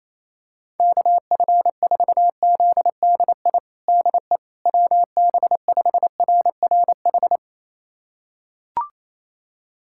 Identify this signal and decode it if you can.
Morse code, heard as a keyed tone, about 28 wpm, 700 Hz. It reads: KF4ZDI DE WB5RRH
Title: USA Callsign DE Callsign - 6 Characters - 28wpm